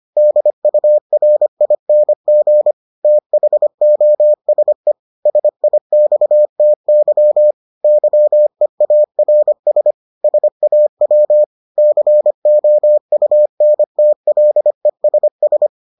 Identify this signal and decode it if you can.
Morse code, heard as a keyed tone, about 25 wpm, 600 Hz. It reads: DURING THOSE SIXTY YEARS SAW COUNTLESS